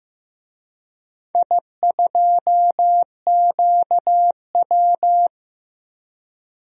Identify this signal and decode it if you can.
Morse code, heard as a keyed tone, about 15 wpm, 700 Hz. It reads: I2QW
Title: DXCC-100 Call Signs - 4 Characters - Repeated 3x - 15wpm